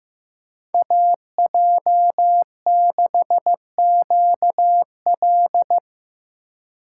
Morse code, heard as a keyed tone, about 15 wpm, 700 Hz. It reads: AJ6QL
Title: K1USN SST QSO Practice - 15wpm